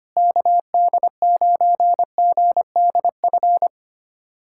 Morse code, heard as a keyed tone, about 25 wpm, 700 Hz. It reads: KD9GDF
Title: USA Callsign DE Callsign 25wpm